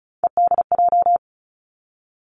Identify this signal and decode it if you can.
Morse code, heard as a keyed tone, about 36 wpm, 700 Hz. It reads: EDJ
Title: Three Challenging Letter-Number Combinations 35wpm